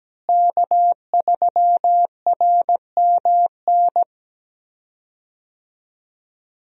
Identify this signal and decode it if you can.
Morse code, heard as a keyed tone, about 17 wpm, 700 Hz. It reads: K3RMN
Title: USA Call Signs - 5 Characters - Repeated 3x - 17wpm